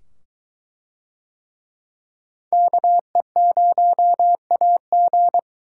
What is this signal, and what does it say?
Morse code, heard as a keyed tone, about 22 wpm, 700 Hz. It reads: KE0AG